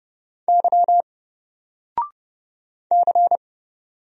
Morse code, heard as a keyed tone, about 30 wpm, 700 Hz. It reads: Y C
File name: Single-Letter-Number-30wpm